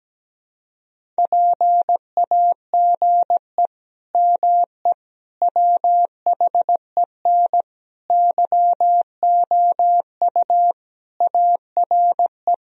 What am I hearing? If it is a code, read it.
Morse code, heard as a keyed tone, about 17 wpm, 700 Hz. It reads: PAGE ME WHEN YOU ARE
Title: Sentences from Top 1000 Words - Review Part 1 - 17wpm